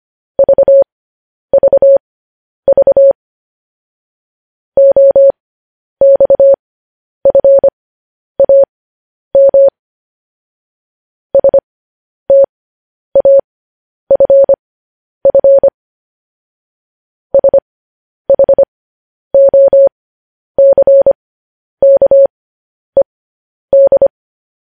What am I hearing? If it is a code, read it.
Morse code, heard as a keyed tone, about 25 wpm, spaced as for 10 wpm, 550 Hz. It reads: VVV OXFAM STAFF SHOCKED
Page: news-headlines-in-morse-code-at-10-wpm